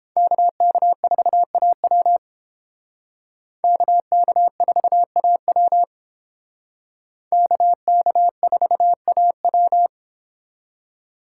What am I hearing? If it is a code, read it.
Morse code, heard as a keyed tone, about 31 wpm, 700 Hz. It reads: KK4AW KK4AW KK4AW